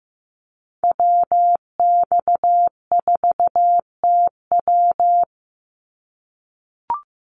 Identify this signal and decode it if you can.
Morse code, heard as a keyed tone, about 15 wpm, 700 Hz. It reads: WX4TW
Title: NA SOTA Activator Callsigns 15wpm